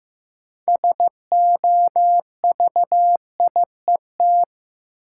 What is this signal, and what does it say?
Morse code, heard as a keyed tone, about 15 wpm, 700 Hz. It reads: SOVIET
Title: Sets of 5 Words Encore - New Word Drill - 15wpm